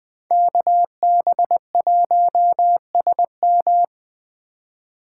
Morse code, heard as a keyed tone, about 20 wpm, 700 Hz. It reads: KB1SM